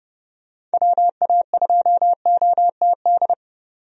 Morse code, heard as a keyed tone, about 32 wpm, 700 Hz. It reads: WA2OTD
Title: USA Call Signs - 6 Characters - 30wpm